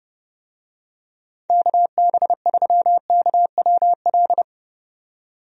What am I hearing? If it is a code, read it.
Morse code, heard as a keyed tone, about 31 wpm, 700 Hz. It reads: KB3KWL
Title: USA Call Signs - 6 Characters - 30wpm